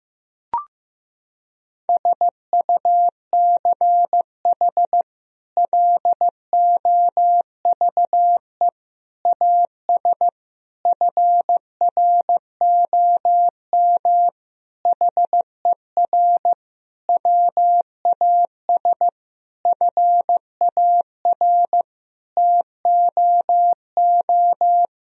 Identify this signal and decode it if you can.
Morse code, heard as a keyed tone, about 15 wpm, 700 Hz. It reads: SUCH LOVE AS FROM HER WAS FAR TOO
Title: Sentences from Top 1000 Words - Review Part 1 - 15wpm